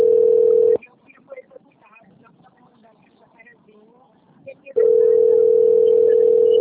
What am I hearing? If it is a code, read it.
Ringback tone (North America), 440 + 480 Hz, with speech.
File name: rg-1600-unknown- - - .5709